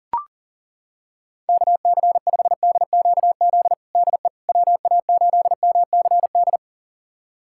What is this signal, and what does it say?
Morse code, heard as a keyed tone, about 40 wpm, 700 Hz. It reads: KC5DQZ DE WA8MCD
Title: USA Callsign DE Callsign 40wpm